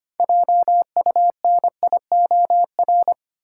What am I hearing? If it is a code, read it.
Morse code, heard as a keyed tone, about 25 wpm, 700 Hz. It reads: JUNIOR
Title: Sentences from Top 1000 Words - Review Part 7 - New Word Drill - 25wpm